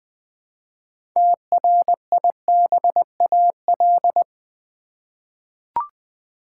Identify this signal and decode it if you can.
Morse code, heard as a keyed tone, about 20 wpm, 700 Hz. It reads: TRIBAL
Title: 6 Letter Words - First 3 Letters Given - 20wpm